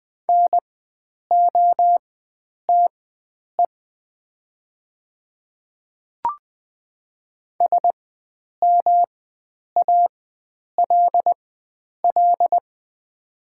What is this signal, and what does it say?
Morse code, heard as a keyed tone, about 20 wpm, spaced as for 9 wpm, 700 Hz. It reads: NOTE SMALL
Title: 4x Character Spacing - Top 500 Words - 20wpm